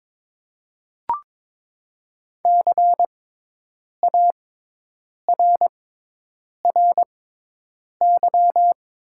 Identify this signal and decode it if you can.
Morse code, heard as a keyed tone, about 22 wpm, spaced as for 8 wpm, 700 Hz. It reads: CARRY
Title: 6x Character Spacing - Top 300 Words - 22wpm